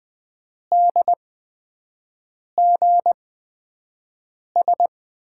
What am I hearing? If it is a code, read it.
Morse code, heard as a keyed tone, about 20 wpm, spaced as for 5 wpm, 700 Hz. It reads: DGS
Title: USA Call Signs - 5 Characters - 8x Character Spacing - 20wpm